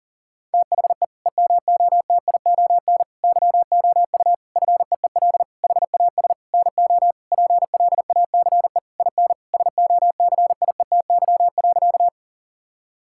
Morse code, heard as a keyed tone, about 37 wpm, 700 Hz. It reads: THE EMOTION YOU FEEL HAS NO PLACE IN SOCIETY.
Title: Sentences from Top 1600 Words 40wpm